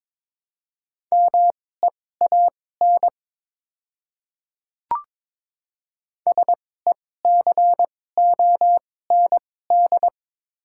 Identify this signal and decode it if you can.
Morse code, heard as a keyed tone, about 22 wpm, spaced as for 16 wpm, 700 Hz. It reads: MEAN SECOND
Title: 2x Character Spacing - Top 300 Words - 22wpm